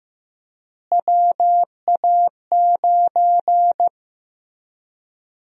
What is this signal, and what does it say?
Morse code, heard as a keyed tone, about 15 wpm, 700 Hz. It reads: WA9
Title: USA Call Sign Prefixes - Repeated 2x - 15wpm